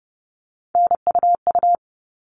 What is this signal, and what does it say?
Morse code, heard as a keyed tone, about 30 wpm, 700 Hz. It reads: NUU
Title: All Cut Numbers - 3-Digit 30wpm